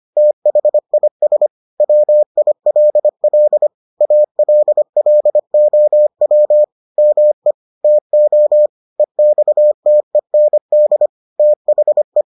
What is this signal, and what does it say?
Morse code, heard as a keyed tone, about 25 wpm, 600 Hz. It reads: THIS WILL ALLOW ME TO EXTEND THE